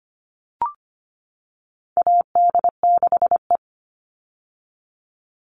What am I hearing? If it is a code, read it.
Morse code, heard as a keyed tone, about 25 wpm, 700 Hz. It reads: AD6E
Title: Call Signs - 25wpm